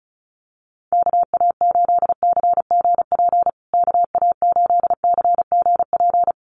Morse code, heard as a keyed tone, about 34 wpm, 700 Hz. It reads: KA8CGP KA8CGP
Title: USA Call Signs Repeated 2x 35wpm